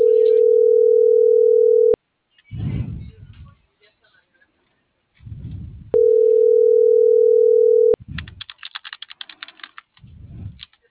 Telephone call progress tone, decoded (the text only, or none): ringback tone